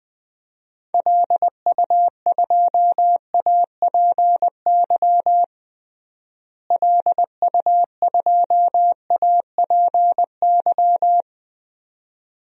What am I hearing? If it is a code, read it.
Morse code, heard as a keyed tone, about 20 wpm, 700 Hz. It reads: LU2APY LU2APY
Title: DXCC-100 Call Signs - 6 Characters - Repeated 2x - 20wpm